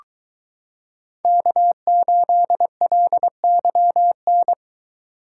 Morse code, heard as a keyed tone, about 22 wpm, 700 Hz. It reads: K8LYN